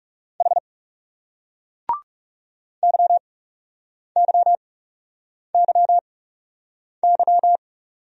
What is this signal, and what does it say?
Morse code, heard as a keyed tone, about 36 wpm, spaced as for 8 wpm, 700 Hz. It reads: H YYYY
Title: Speed-Racing Single Letters 30wpm